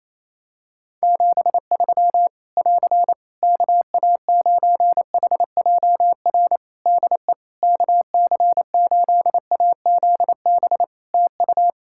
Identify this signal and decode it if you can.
Morse code, heard as a keyed tone, about 27 wpm, 700 Hz. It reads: 73 <AR> KA9HJR DE KC8AZB TU
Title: Practice QSO - Exchange 4 - Ending - 28wpm